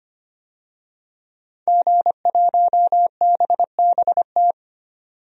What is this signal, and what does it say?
Morse code, heard as a keyed tone, about 25 wpm, 700 Hz. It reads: G1BBT
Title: DXCC-100 Call Signs - 5 Characters - 25wpm